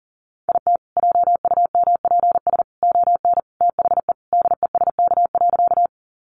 Morse code, heard as a keyed tone, about 40 wpm, 700 Hz. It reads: IT JUMPS ON THE DESK.
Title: Additional Easy Phrases 40wpm